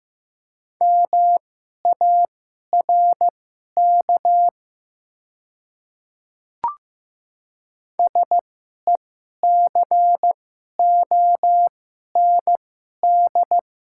Morse code, heard as a keyed tone, about 15 wpm, spaced as for 11 wpm, 700 Hz. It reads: MARK SECOND
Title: 2x Character Spacing - Top 500 Words - 15wpm